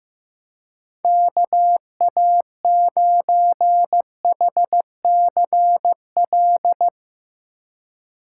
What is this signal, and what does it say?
Morse code, heard as a keyed tone, about 16 wpm, 700 Hz. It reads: KA9HCL
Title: Speed Racing - USA 2x3 Call Signs 15wpm